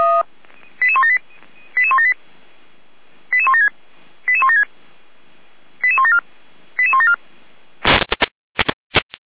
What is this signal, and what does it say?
Five-tone selective call (ZVEI), heard as quick runs of five tones, tones of about 70 ms. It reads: 88178 88178 88176 88176 88174 88174